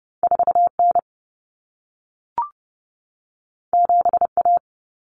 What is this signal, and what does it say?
Morse code, heard as a keyed tone, about 30 wpm, 700 Hz. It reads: 4N 7A
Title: Common Cut Numbers - 2-Digit 30wpm